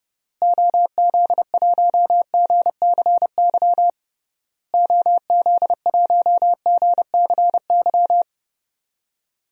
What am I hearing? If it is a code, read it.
Morse code, heard as a keyed tone, about 32 wpm, 700 Hz. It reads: OZ1GCY OZ1GCY
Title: DXCC-100 Call Signs - 6 Characters - Repeated 3x - 30wpm